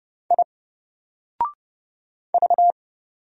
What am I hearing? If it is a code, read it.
Morse code, heard as a keyed tone, about 30 wpm, 700 Hz. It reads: I V
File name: Single-Letter-Number-30wpm